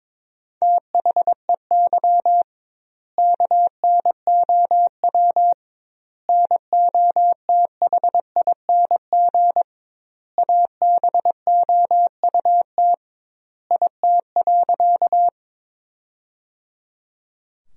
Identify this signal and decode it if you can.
Morse code, heard as a keyed tone, about 22 wpm, 700 Hz. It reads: THEY KNOW NOTHING ABOUT IT.
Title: 2x Word-Spacing - Sentences from Top 500 Words - 22wpm